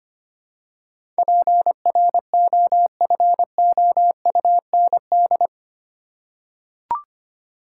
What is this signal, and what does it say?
Morse code, heard as a keyed tone, about 25 wpm, 700 Hz. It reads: PROFOUND
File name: Eight Letter Words - 25wpm